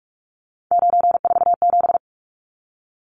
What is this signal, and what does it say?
Morse code, heard as a keyed tone, about 44 wpm, 700 Hz. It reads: 947
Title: Numbers - 3-Digit 45wpm